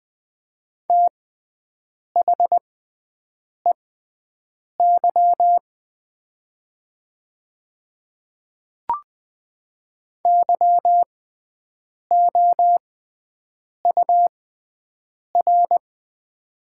Morse code, heard as a keyed tone, about 20 wpm, spaced as for 7 wpm, 700 Hz. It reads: THEY YOUR
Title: 6x Character Spacing - Top 200 Words - 20wpm